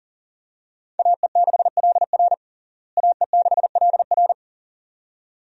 Morse code, heard as a keyed tone, about 40 wpm, 700 Hz. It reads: AE6LR AE6LR
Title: USA Call Signs - 5 Characters - Repeated 2x - 40wpm